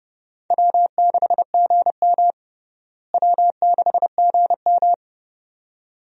Morse code, heard as a keyed tone, about 30 wpm, 700 Hz. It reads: W6GM W6GM
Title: USA Call Signs - 4 Characters - Repeated 2x - 30wpm